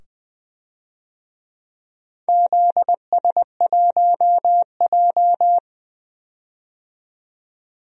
Morse code, heard as a keyed tone, about 20 wpm, 700 Hz. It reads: ZS1J
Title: DXCC-100 Call Signs - 4 Characters - 20wpm